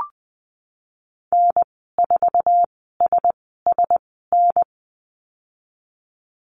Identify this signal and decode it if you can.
Morse code, heard as a keyed tone, about 20 wpm, spaced as for 15 wpm, 700 Hz. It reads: N4SSN
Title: 062 Lesson 11g - Letter h - Call Signs 20wpm